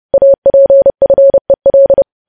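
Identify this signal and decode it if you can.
Morse code, heard as a keyed tone, about 30 wpm, 550 Hz. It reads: APFEL
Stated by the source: Morse code practice of news headlines at 30.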